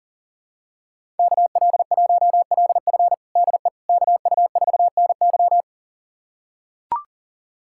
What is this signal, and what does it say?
Morse code, heard as a keyed tone, about 40 wpm, 700 Hz. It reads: KL1LF DE KU4NY
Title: USA Callsign DE Callsign - 5 Characters - 40wpm